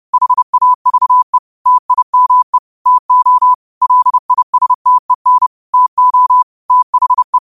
Morse code, heard as a keyed tone, about 30 wpm, 1000 Hz. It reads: HAVE TIME TO LISTEN TO THE
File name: qotdfiles / 30wpm